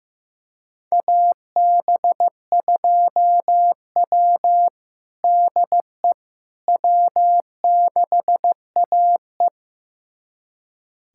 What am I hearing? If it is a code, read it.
Morse code, heard as a keyed tone, about 16 wpm, 700 Hz. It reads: AB2W DE W6AE
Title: USA Callsign DE Callsign - 4 Characters - 15wpm